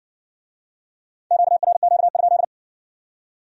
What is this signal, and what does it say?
Morse code, heard as a keyed tone, about 56 wpm, 700 Hz. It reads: QRQ?